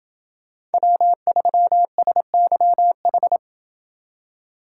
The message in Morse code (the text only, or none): W3SYH